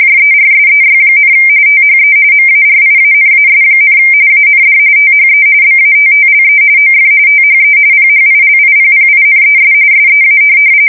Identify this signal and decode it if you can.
RTTY, about 45.45 baud. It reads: JERSEY, THE MARIANAS, AND THE BATTLE OF THE PHILIPPINES SEA
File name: fsk170 / USS-New-Jersey